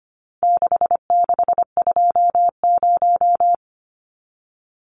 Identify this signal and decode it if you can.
Morse code, heard as a keyed tone, about 25 wpm, 700 Hz. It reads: 6620
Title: Numbers - 4-Digit 25wpm